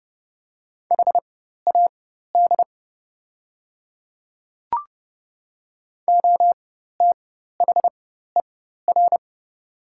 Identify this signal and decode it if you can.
Morse code, heard as a keyed tone, about 30 wpm, spaced as for 14 wpm, 700 Hz. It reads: HAD OTHER
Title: 4x Character Spacing - Top 100 Words - 30wpm